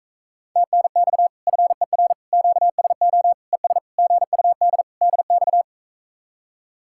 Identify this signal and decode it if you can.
Morse code, heard as a keyed tone, about 45 wpm, 700 Hz. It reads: TNX FER QSO ES GUD DX